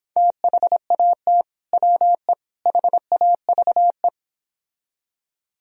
Morse code, heard as a keyed tone, about 26 wpm, 700 Hz. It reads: THAT WE HAVE